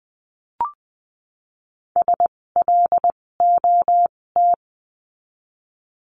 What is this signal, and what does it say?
Morse code, heard as a keyed tone, about 20 wpm, spaced as for 16 wpm, 700 Hz. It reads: SLOT
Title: 256 Lesson 42a - Words 12-20wpm